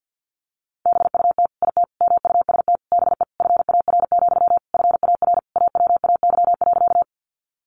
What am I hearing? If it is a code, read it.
Morse code, heard as a keyed tone, about 50 wpm, 700 Hz. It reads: BUT IT MUST BE FAR, FAR AWAY.
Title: Sentences from Top 600 Words 50wpm